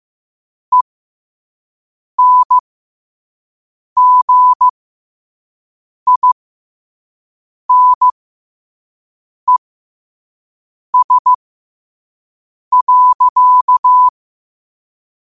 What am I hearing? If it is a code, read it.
Morse code, heard as a keyed tone, about 15 wpm, spaced as for 5 wpm, 1000 Hz. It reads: ENGINES.
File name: qotdfiles / 5wpm